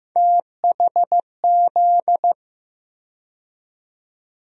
Morse code, heard as a keyed tone, about 15 wpm, 700 Hz. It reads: THZ
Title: USA Call Sign Postfixes - 3 Letters 15wpm